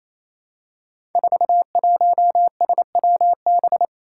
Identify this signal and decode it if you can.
Morse code, heard as a keyed tone, about 28 wpm, 700 Hz. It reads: 41SWB
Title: Five Challenging Letter-Number Combinations 28wpm